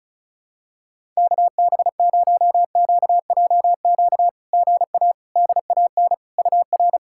Morse code, heard as a keyed tone, about 36 wpm, 700 Hz. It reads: KB0QJQ GA DAN UR
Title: SOTA QSO Practice 35wpm